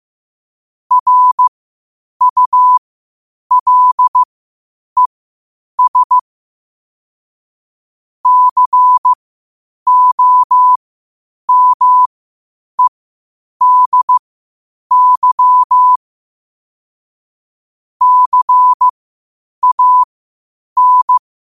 Morse code, heard as a keyed tone, about 15 wpm, spaced as for 8 wpm, 1000 Hz. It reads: RULES COMEDY CAN
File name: qotdfiles / 8wpm